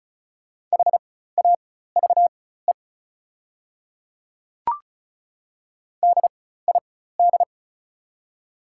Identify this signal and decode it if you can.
Morse code, heard as a keyed tone, about 36 wpm, spaced as for 16 wpm, 700 Hz. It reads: HAVE DID
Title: 4x Character Spacing - Top 100 Words - 35wpm